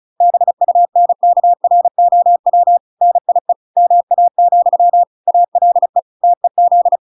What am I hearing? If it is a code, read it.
Morse code, heard as a keyed tone, about 34 wpm, 700 Hz. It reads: BUNKROW NIE MA, ALE TEZ